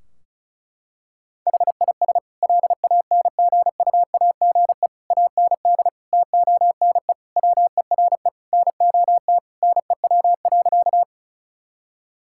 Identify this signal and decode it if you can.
Morse code, heard as a keyed tone, about 36 wpm, 700 Hz. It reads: HIS LANGUAGE AND TONE WERE NOT NEW.
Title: Sentences from Top 1600 Words 35wpm